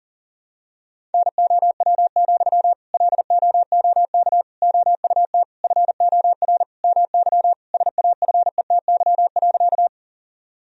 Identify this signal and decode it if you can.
Morse code, heard as a keyed tone, about 40 wpm, 700 Hz. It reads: NOW, LOOK OUT FOR MY SAFETY.
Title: Sentences from Top 1000 Words - Review Part 4 - 40wpm